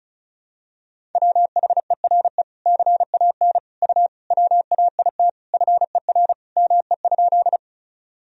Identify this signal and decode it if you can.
Morse code, heard as a keyed tone, about 36 wpm, 700 Hz. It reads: WHERE CAN U WAIT FER ME?
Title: Sentences from Top 500 Words and Abbreviations 35wpm